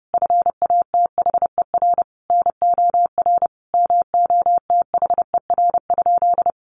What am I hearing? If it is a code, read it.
Morse code, heard as a keyed tone, about 30 wpm, 700 Hz. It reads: FATHER NOR MOTHER?
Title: Sentences from Top 1500 Words 30wpm